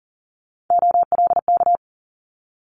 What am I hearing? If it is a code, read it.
Morse code, heard as a keyed tone, about 42 wpm, 700 Hz. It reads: OLK
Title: Multi-letter Phonemes Quick Fire 40wpm